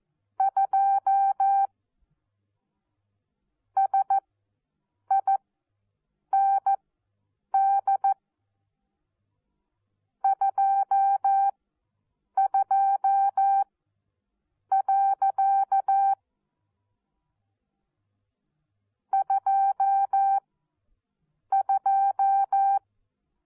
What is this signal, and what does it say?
Morse code, heard as a keyed tone, about 14 wpm, spaced as for 7 wpm, 780 Hz. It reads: 2 SIND 22. 22
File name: Onlinelehrgang / morsen / DJ4UF Morsen14